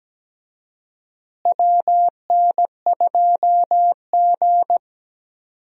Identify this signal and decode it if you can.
Morse code, heard as a keyed tone, about 17 wpm, 700 Hz. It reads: WN2G
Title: USA Call Signs - 4 Characters - Repeated 2x - 17wpm